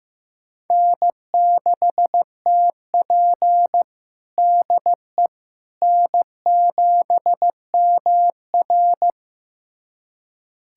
Morse code, heard as a keyed tone, about 15 wpm, 700 Hz. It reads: N6TP DE N7MR
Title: USA Callsign DE Callsign - 4 Characters - 15wpm